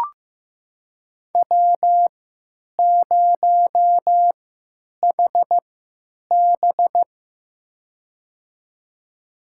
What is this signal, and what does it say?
Morse code, heard as a keyed tone, about 15 wpm, spaced as for 9 wpm, 700 Hz. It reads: W0HB
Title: USA Call Signs - 4 Characters - 3x Character Spacing - 15wpm